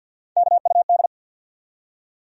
Morse code, heard as a keyed tone, about 53 wpm, 700 Hz. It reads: KUD
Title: USA Call Sign Postfixes - 3 Letters 50wpm